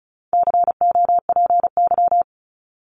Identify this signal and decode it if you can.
Morse code, heard as a keyed tone, about 34 wpm, 700 Hz. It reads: COPY
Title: Top 1251-1500 Words 35wpm